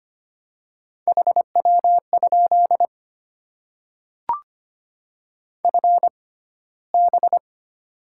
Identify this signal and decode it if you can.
Morse code, heard as a keyed tone, about 25 wpm, spaced as for 15 wpm, 700 Hz. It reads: HW? FB
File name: Top 20 QSO Elements - 6x Character Spacing - 25wpm